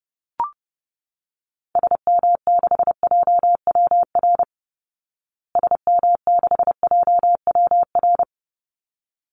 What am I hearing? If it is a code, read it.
Morse code, heard as a keyed tone, about 30 wpm, 700 Hz. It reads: SM6JWR SM6JWR
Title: Call Signs Repeated 3x - 30wpm